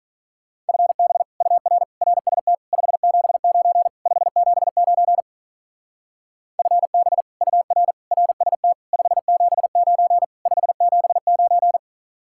Morse code, heard as a keyed tone, about 44 wpm, 700 Hz. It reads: FB UR RST 579 579 FB UR RST 579 579